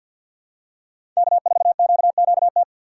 Morse code, heard as a keyed tone, about 53 wpm, 700 Hz. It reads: K4QQT
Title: USA Call Signs - 5 Characters - 6x Character Spacing - 50wpm